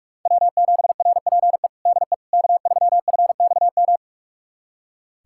Morse code, heard as a keyed tone, about 44 wpm, 700 Hz. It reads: W7RPE DE K3FXM
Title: USA Callsign DE Callsign - 5 Characters - 45wpm